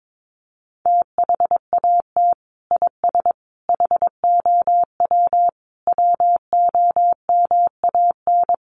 Morse code, heard as a keyed tone, about 22 wpm, 700 Hz. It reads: THAT IS HOW WOMAN
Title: Sentences from Top 1100 Words 22wpm